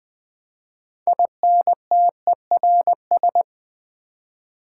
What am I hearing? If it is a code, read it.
Morse code, heard as a keyed tone, about 20 wpm, 700 Hz. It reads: INTERS
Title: Build A Word - 6 Letters At A Time - 12-Letter Words - 20wpm